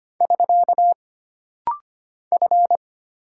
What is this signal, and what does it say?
Morse code, heard as a keyed tone, about 25 wpm, 700 Hz. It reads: <SK> F
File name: Single Character (Letter:Number:Punctuation:Prosign) - Rapid-Fire - 25wpm